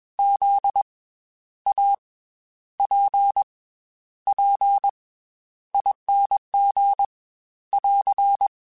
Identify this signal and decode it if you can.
Morse code, heard as a keyed tone, about 22 wpm, 800 Hz. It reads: Z A P P ING <AR>
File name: Special INGs / Special INGs-mp3s / Slow ing @20